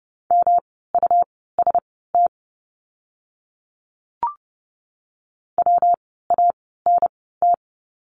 Morse code, heard as a keyed tone, about 30 wpm, spaced as for 17 wpm, 700 Hz. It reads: MUST WANT
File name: course / 30:10wpm / 237 Lesson 38f - Letter q - Words 30wpm